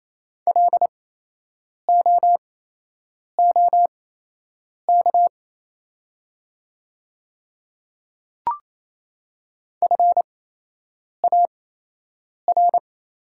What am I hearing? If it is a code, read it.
Morse code, heard as a keyed tone, about 28 wpm, spaced as for 8 wpm, 700 Hz. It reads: LOOK FAR
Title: 8x Character Spacing - Top 300 Words - 28wpm